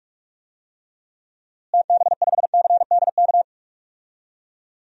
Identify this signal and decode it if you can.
Morse code, heard as a keyed tone, about 46 wpm, 700 Hz. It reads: TB5CDK
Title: DXCC-100 Call Signs - 6 Characters - Repeated 2x - 45wpm